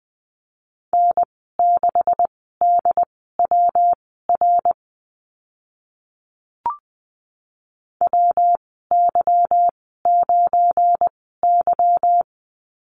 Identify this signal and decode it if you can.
Morse code, heard as a keyed tone, about 20 wpm, spaced as for 15 wpm, 700 Hz. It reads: N6DWR WY9Y
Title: 207 Lesson 33g - Letter j - Call Signs 20wpm